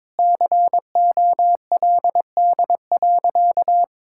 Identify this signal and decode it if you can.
Morse code, heard as a keyed tone, about 22 wpm, 700 Hz. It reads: COLD.
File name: practice / 22wpm / Sentences with 3-Word Phrases - 22wpm